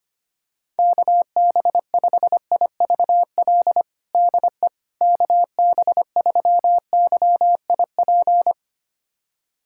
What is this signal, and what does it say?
Morse code, heard as a keyed tone, about 25 wpm, 700 Hz. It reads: KB5IVL DE KB3YIP
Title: USA Callsign DE Callsign - 6 Characters - 25wpm